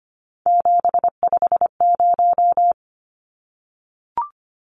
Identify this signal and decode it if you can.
Morse code, heard as a keyed tone, about 25 wpm, 700 Hz. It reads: 750
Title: Numbers - 3-Digit 25wpm